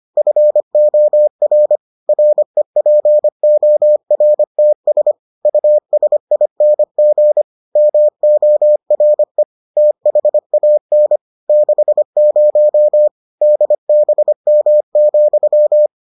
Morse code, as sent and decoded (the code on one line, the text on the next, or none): ..-. --- .-. / .-. .--. --- .-. - ... / ..- ... .. -. --. / -- --- .-. / - .... .- -. / -.... ----- / -.. -... -- --..--
FOR REPORTS USING MORE THAN 60 DBM,